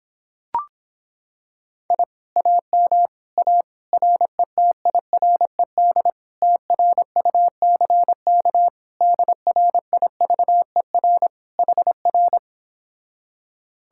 Morse code, heard as a keyed tone, about 26 wpm, 700 Hz. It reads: I AM A RETIRED TRUCK DRIVER HR